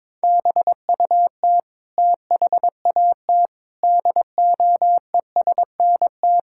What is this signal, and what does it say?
Morse code, heard as a keyed tone, about 22 wpm, 700 Hz. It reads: BUT THAT DOESNT
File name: Sentences with 7-Word Phrases - 22wpm